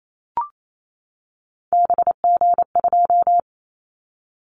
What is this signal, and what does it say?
Morse code, heard as a keyed tone, about 28 wpm, 700 Hz. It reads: BG2
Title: Callsign Prefixes for Top 100 DXCC - 28wpm